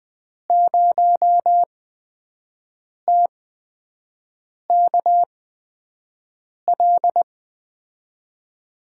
Morse code, heard as a keyed tone, about 21 wpm, spaced as for 6 wpm, 700 Hz. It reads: 0TKL
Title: USA Call Signs - 6 Characters - 8x Character Spacing - 20wpm